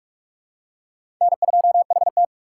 Morse code, heard as a keyed tone, about 46 wpm, 700 Hz. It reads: N2HT
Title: USA Call Signs - 4 Characters - 6x Character Spacing - 45wpm